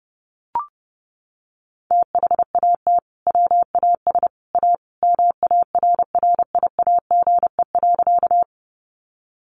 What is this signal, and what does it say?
Morse code, heard as a keyed tone, about 30 wpm, 700 Hz. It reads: THAT WAS A MARRIAGE.
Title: Sentences from Top 800 Words 30wpm